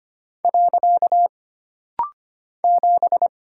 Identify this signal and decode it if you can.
Morse code, heard as a keyed tone, about 25 wpm, 700 Hz. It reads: . 7
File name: Single Character (Letter:Number:Punctuation:Prosign) - Rapid-Fire - 25wpm